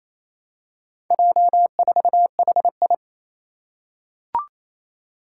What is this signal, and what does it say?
Morse code, heard as a keyed tone, about 28 wpm, 700 Hz. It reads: J4HI
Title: Four Challenging Letter-Number Combinations 28wpm